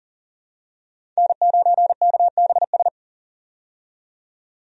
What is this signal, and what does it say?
Morse code, heard as a keyed tone, about 40 wpm, 700 Hz. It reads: N9KBS
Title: USA Call Signs - 5 Characters - 40wpm